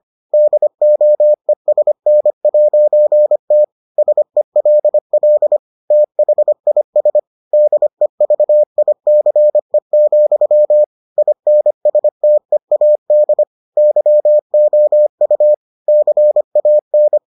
Morse code, heard as a keyed tone, about 25 wpm, 600 Hz. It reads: DOESN'T SELL THIS DEVICE, INSTEAD YOU CAN